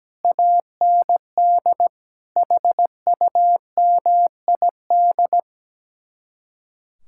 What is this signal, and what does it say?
Morse code, heard as a keyed tone, about 17 wpm, 700 Hz. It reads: AND HUMID
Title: Speed-Racing Common QSO Phrases - 17wpm